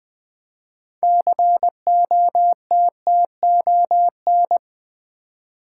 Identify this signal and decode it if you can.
Morse code, heard as a keyed tone, about 21 wpm, 700 Hz. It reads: COTTON
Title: 6 Letter Words - First 3 Letters Given - 20wpm